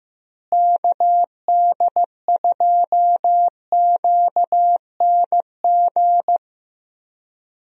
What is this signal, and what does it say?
Morse code, heard as a keyed tone, about 15 wpm, 700 Hz. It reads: KD2QNG
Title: USA Call Signs - 6 Characters - Repeated 3x - 15wpm